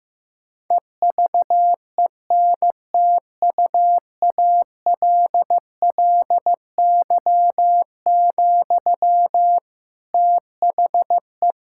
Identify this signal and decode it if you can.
Morse code, heard as a keyed tone, about 15 wpm, 700 Hz. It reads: EVENTUALLY, THE
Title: Sentences from Top 1000 Words - Review Part 6 - 15wpm